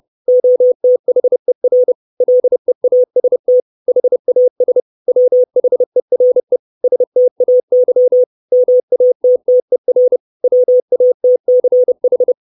Morse code, heard as a keyed tone, about 30 wpm, 500 Hz. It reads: OTHER LEAST HAS WHERE STAY MATTER WATCH
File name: 632 Words 30WPM